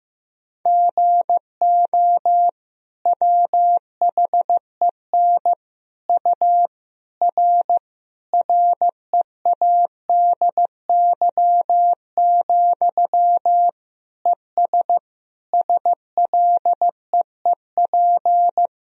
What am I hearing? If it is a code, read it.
Morse code, heard as a keyed tone, about 15 wpm, 700 Hz. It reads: GO WHEN U R READY, ES SLEEP
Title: Sentences from Top 500 Words and Abbreviations 15wpm